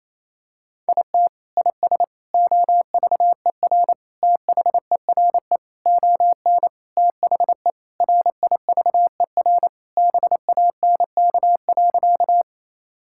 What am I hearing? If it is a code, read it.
Morse code, heard as a keyed tone, about 28 wpm, 700 Hz. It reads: IT IS OVER THERE ON THE RIVER BANK.
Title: Sentences from Top 1000 Words - Review Part 6 - 28wpm